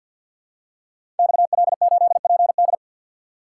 Morse code, heard as a keyed tone, about 52 wpm, 700 Hz. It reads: XL8PD
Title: Code Groups - 5 Letter:Numbers - 50wpm